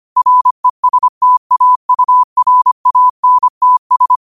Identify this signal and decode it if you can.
Morse code, heard as a keyed tone, about 25 wpm, 1000 Hz. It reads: RESTAURANTS
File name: 25wpm